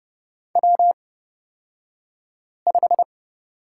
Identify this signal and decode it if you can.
Morse code, heard as a keyed tone, about 30 wpm, spaced as for 5 wpm, 700 Hz. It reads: W5
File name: course / 30:10wpm / 235 Lesson 38d - Letter q - Two Characters 30wpm